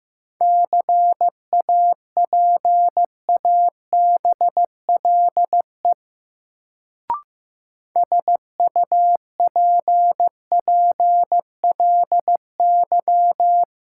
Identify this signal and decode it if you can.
Morse code, heard as a keyed tone, about 15 wpm, 700 Hz. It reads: CAPABLE SUPPLY
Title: Sentences from Top 1200 Words - New Word Drill Alt - 15wpm